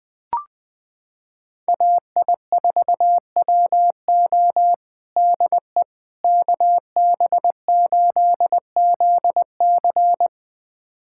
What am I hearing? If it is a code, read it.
Morse code, heard as a keyed tone, about 20 wpm, 700 Hz. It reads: AI4WO DE KB8ZC
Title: USA Callsign DE Callsign - 5 Characters - 20wpm